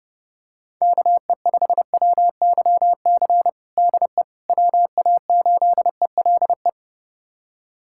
Morse code, heard as a keyed tone, about 32 wpm, 700 Hz. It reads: KE5WYC DE WA8ELE
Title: USA Callsign DE Callsign - 6 Characters - 30wpm